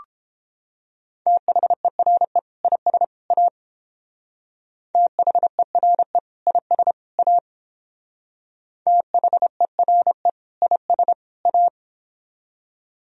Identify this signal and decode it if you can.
Morse code, heard as a keyed tone, about 28 wpm, 700 Hz. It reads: THERE IS A THERE IS A THERE IS A